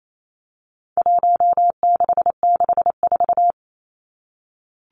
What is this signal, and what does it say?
Morse code, heard as a keyed tone, about 28 wpm, 700 Hz. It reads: 1664
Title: Numbers - Contest Style - 4-Digit 28wpm